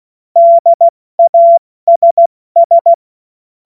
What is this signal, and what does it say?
Morse code, heard as a keyed tone, about 16 wpm, 680 Hz. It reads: DASS
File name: Morsekurs / Witze / Witz 06. September 2022 70 F80